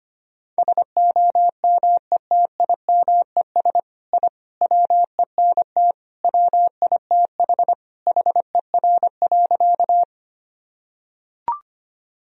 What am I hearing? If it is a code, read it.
Morse code, heard as a keyed tone, about 25 wpm, 700 Hz. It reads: SOMETIMES I WENT WITH HER.
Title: Sentences from Top 1000 Words - Review Part 3 - 25wpm